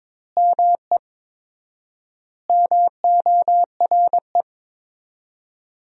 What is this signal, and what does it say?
Morse code, heard as a keyed tone, about 22 wpm, 700 Hz. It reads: ME MORE
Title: 4x Word-Spacing - Sentences from Top 200 Words - 22wpm